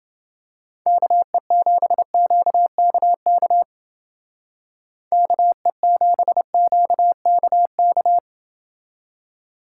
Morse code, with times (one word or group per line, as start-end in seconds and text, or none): 0.9-3.6 KE7QKK
5.1-8.2 KE7QKK